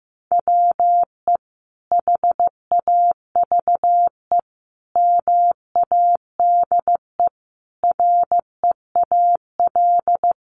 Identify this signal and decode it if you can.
Morse code, heard as a keyed tone, about 15 wpm, 700 Hz. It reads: WE HAVE MADE REAL
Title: Sentences from Top 1400 Words 15wpm